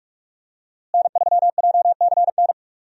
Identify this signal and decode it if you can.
Morse code, heard as a keyed tone, about 46 wpm, 700 Hz. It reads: N3JCN
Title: USA Call Signs - 5 Characters - Repeated 2x - 45wpm